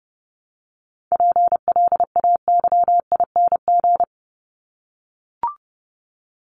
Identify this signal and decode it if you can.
Morse code, heard as a keyed tone, about 30 wpm, 700 Hz. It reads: PLAYING
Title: ING Suffix - Top 100 Words with Base Words 30wpm